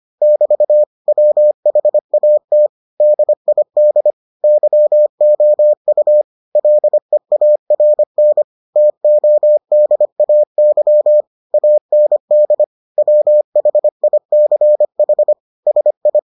Morse code, heard as a keyed tone, about 25 wpm, 600 Hz. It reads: <BT> WHAT DID YOU LEARN TODAY AND WHICH SI